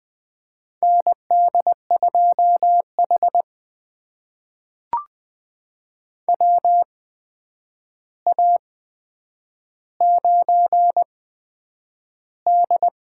Morse code, heard as a keyed tone, about 20 wpm, spaced as for 9 wpm, 700 Hz. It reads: ND2H WA9D
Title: USA Call Signs - 4 Characters - 8x Character Spacing - 20wpm